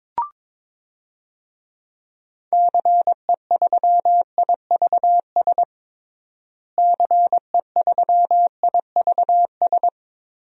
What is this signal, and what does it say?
Morse code, heard as a keyed tone, about 22 wpm, 700 Hz. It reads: CE3IVS CE3IVS
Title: DXCC-100 Call Signs - 6 Characters - Repeated 2x - 22wpm